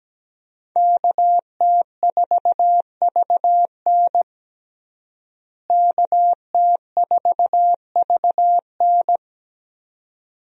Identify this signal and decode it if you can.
Morse code, heard as a keyed tone, about 17 wpm, 700 Hz. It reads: KT4VN KT4VN
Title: USA Call Signs - 5 Characters - Repeated 2x - 17wpm